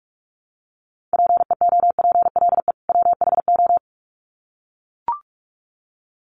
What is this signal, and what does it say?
Morse code, heard as a keyed tone, about 44 wpm, 700 Hz. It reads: PEOPLE WHO
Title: Sets of 2 Words 45wpm